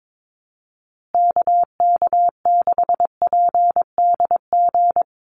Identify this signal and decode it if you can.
Morse code, heard as a keyed tone, about 22 wpm, 700 Hz. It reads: KK6PDG
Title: USA Call Signs Repeated 3x 22wpm